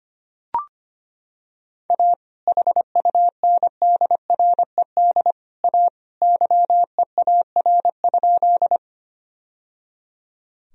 Morse code, heard as a keyed tone, about 25 wpm, 700 Hz. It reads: A HUNDRED A YEAR?
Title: Sentences from Top 1000 Words - Review Part 2 - 25wpm